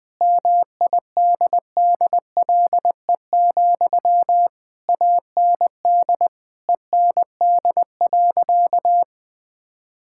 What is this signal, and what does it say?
Morse code, heard as a keyed tone, about 21 wpm, 700 Hz. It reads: MIDDLE, AND END.
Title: Sentences from Top 1000 Words - Review Part 6 - 20wpm